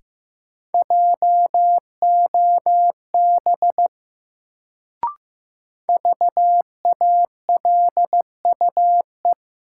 Morse code, heard as a keyed tone, about 15 wpm, 700 Hz. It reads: JOB VALUE
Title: Sentences from Top 600 Words - New Word Drill - 15wpm